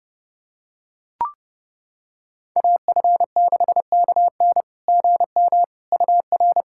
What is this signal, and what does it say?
Morse code, heard as a keyed tone, about 30 wpm, 690 Hz. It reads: AF6KN GM UR
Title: SOTA QSO Practice 30wpm